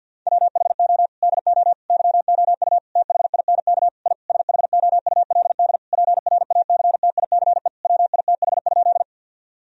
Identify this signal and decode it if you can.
Morse code, heard as a keyed tone, about 50 wpm, 700 Hz. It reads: WHO DO YOU THINK I SHOULD PRACTICE WITH?
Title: Sentences from Top 1000 Words - Review Part 4 - 50wpm